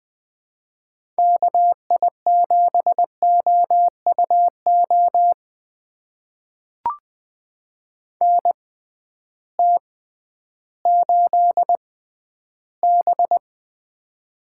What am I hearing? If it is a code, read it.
Morse code, heard as a keyed tone, about 20 wpm, spaced as for 12 wpm, 700 Hz. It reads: KI7OUO NT8B
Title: USA Call Signs - 6x Character Spacing - 20wpm